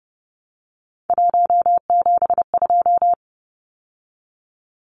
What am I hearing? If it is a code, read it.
Morse code, heard as a keyed tone, about 30 wpm, 700 Hz. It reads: 172
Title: Numbers - 3-Digit 30wpm